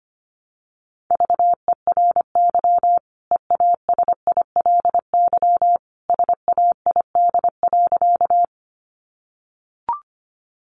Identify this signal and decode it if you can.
Morse code, heard as a keyed tone, about 25 wpm, 700 Hz. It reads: VERY EASILY SAID.
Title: Sentences from Top 900 Words 25wpm